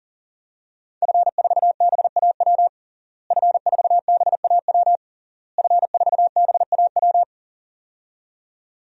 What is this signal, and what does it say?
Morse code, heard as a keyed tone, about 40 wpm, 700 Hz. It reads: F4BAW F4BAW F4BAW
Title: DXCC-100 Call Signs - 5 Characters - Repeated 3x - 40wpm